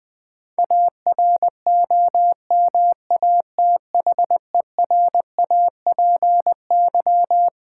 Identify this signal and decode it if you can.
Morse code, heard as a keyed tone, about 20 wpm, 690 Hz. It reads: AROMATHERAPY
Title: Difficult Words with Hints - 20wpm